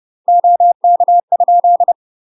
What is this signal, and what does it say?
Morse code, heard as a keyed tone, about 30 wpm, 700 Hz. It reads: OK?
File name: UK MASTER QSOs TRAINING/30WPM